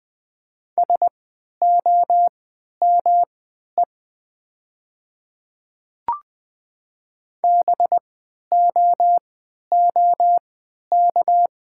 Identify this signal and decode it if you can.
Morse code, heard as a keyed tone, about 20 wpm, spaced as for 11 wpm, 700 Hz. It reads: SOME BOOK
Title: 3x Character Spacing - Top 300 Words - 20wpm